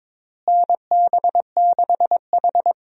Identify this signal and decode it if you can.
Morse code, heard as a keyed tone, about 22 wpm, 700 Hz. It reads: NB6H
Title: USA Call Signs - 4 Characters - Repeated 3x - 22wpm